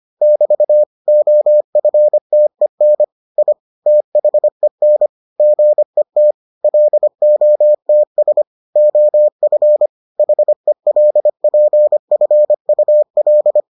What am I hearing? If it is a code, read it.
Morse code, heard as a keyed tone, about 25 wpm, 600 Hz. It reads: <BT> OFTEN I THEN GET LOTS OF HELPFUL